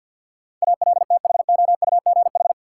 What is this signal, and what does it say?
Morse code, heard as a keyed tone, about 52 wpm, 700 Hz. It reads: ALTHOUGH
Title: Sentences from Top 600 Words - New Word Drill - 50wpm